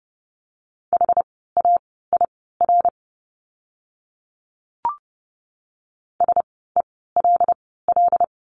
Morse code, heard as a keyed tone, about 30 wpm, spaced as for 17 wpm, 700 Hz. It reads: HAIR SELL
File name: Lesson 17f - Letter u - Words 30wpm